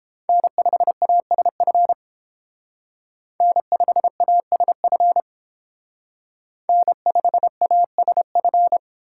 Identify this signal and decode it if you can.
Morse code, heard as a keyed tone, about 31 wpm, 700 Hz. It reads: N5ASF N5ASF N5ASF